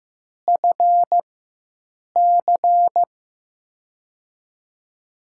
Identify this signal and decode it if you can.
Morse code, heard as a keyed tone, about 15 wpm, spaced as for 7 wpm, 700 Hz. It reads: FC
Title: USA Call Signs - 4x Character Spacing - 15wpm